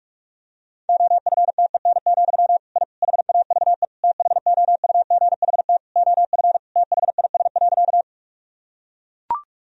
Morse code, heard as a keyed tone, about 41 wpm, 700 Hz. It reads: OFTEN, I HAVE THOUGHT OF THIS.
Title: Sentences from Top 1000 Words - Review Part 1 - 45wpm